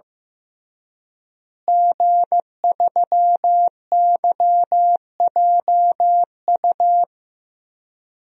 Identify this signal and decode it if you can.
Morse code, heard as a keyed tone, about 16 wpm, 700 Hz. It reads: G3YJU
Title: DXCC-100 Call Signs - 5 Characters - Repeated 3x - 15wpm